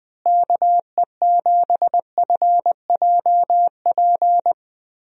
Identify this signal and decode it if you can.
Morse code, heard as a keyed tone, about 20 wpm, 700 Hz. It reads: KE7FJP
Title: USA Call Signs - 6 Characters - Repeated 3x - 20wpm